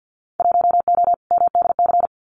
Morse code, heard as a keyed tone, about 50 wpm, 700 Hz. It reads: 1O MDC
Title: Winter Field Day Exchanges 50wpm